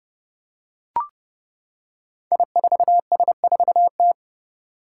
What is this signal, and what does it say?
Morse code, heard as a keyed tone, about 29 wpm, 700 Hz. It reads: I4S4T
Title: Five Challenging Letter-Number Combinations 30wpm